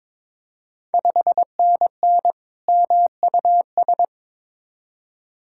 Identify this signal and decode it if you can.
Morse code, heard as a keyed tone, about 22 wpm, 700 Hz. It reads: 5NN MUS
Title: Ontario QSO Party Exchange 22wpm